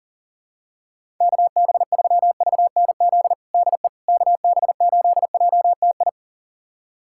Morse code, heard as a keyed tone, about 38 wpm, 700 Hz. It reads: KB3VNZ DE KB8JTI
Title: USA Callsign DE Callsign 40wpm